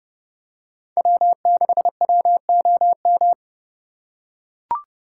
Morse code, heard as a keyed tone, about 29 wpm, 700 Hz. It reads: W6WOM